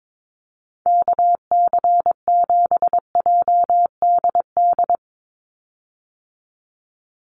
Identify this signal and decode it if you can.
Morse code, heard as a keyed tone, about 22 wpm, 700 Hz. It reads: KC7JDD
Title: USA Call Signs 22wpm